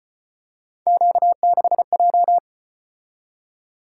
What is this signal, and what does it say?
Morse code, heard as a keyed tone, about 34 wpm, 700 Hz. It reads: Q6J